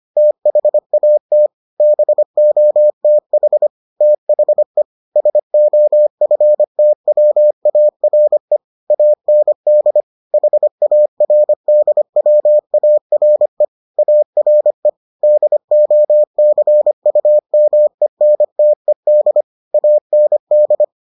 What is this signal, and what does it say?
Morse code, heard as a keyed tone, about 25 wpm, 600 Hz. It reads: THAT BOTH THE SOFTWARE AND HARDWARE ARE DOCUMENTED AND